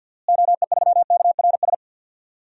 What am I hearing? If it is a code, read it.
Morse code, heard as a keyed tone, about 53 wpm, 700 Hz. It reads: OE3KRS
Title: DXCC-100 Call Signs - 6 Characters - Repeated 2x - 50wpm